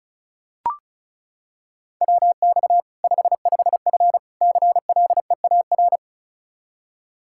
Morse code, heard as a keyed tone, about 36 wpm, 700 Hz. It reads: WX 55F CLEAR
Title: Practice QSO - Exchange 2 - Info - 35wpm